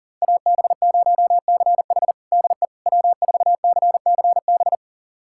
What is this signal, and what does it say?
Morse code, heard as a keyed tone, about 40 wpm, 700 Hz. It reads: AB0CH DE W4CCB
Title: USA Callsign DE Callsign - 5 Characters - 40wpm